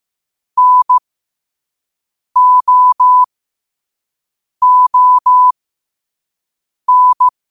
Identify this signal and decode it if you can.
Morse code, heard as a keyed tone, about 15 wpm, spaced as for 5 wpm, 1000 Hz. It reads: NOON
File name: qotdfiles / 5wpm